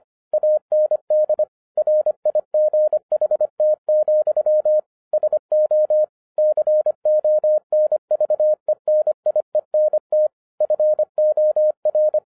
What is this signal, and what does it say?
Morse code, heard as a keyed tone, about 25 wpm, 610 Hz. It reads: AND RIGHT, SO CONVENIENT FOR